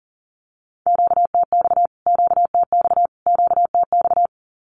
Morse code, heard as a keyed tone, about 40 wpm, 700 Hz. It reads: QTX QTX QTX
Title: Q-Codes Repeated 3x 40wpm